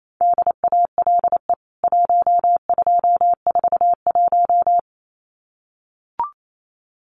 Morse code, heard as a keyed tone, about 28 wpm, 700 Hz. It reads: DALE 1241
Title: CWT Exchange 28wpm